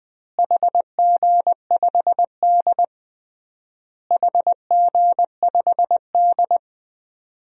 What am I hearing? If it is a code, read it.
Morse code, heard as a keyed tone, about 20 wpm, 700 Hz. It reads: HG5D HG5D
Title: DXCC-100 Call Signs - 4 Characters - Repeated 3x - 20wpm